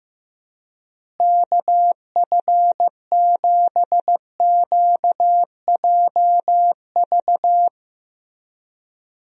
Morse code, heard as a keyed tone, about 15 wpm, 700 Hz. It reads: KF7QJV
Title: USA Call Signs - 6 Characters - Repeated 3x - 15wpm